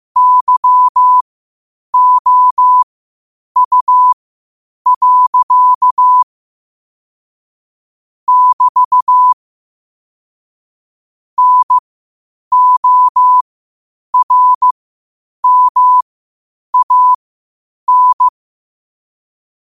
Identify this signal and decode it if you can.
Morse code, heard as a keyed tone, about 15 wpm, spaced as for 8 wpm, 1000 Hz. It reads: YOU. <BT> NORMAN
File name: qotdfiles / 8wpm